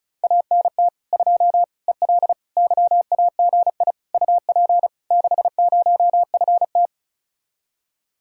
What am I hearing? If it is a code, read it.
Morse code, heard as a keyed tone, about 43 wpm, 700 Hz. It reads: ANT 2 EL YAGI UP 60FT
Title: Speed-Racing Common QSO Phrases - 35wpm